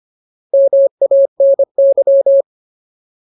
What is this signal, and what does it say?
Morse code, heard as a keyed tone, about 25 wpm, 550 Hz. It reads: MANY
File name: Ultimate Word List / 25WPM Ultimate